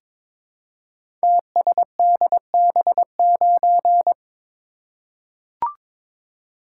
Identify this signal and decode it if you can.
Morse code, heard as a keyed tone, about 22 wpm, 700 Hz. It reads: TSDB9
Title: Code Groups - 5 Letter:Numbers - 22wpm